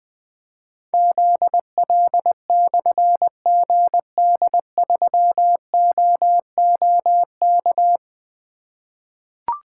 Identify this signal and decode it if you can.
Morse code, heard as a keyed tone, about 20 wpm, 700 Hz. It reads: ZL/GD3OOK
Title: Difficult Call Signs 20wpm